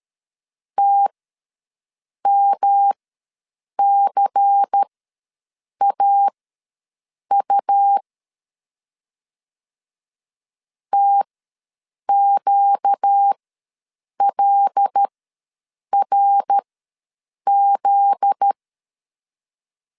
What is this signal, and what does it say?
Morse code, heard as a keyed tone, about 13 wpm, spaced as for 7 wpm, 800 Hz. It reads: TMCAU TQLRZ
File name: LEZIONI / 7 GF9Z / 7 GF9Z ws 12wpm 3321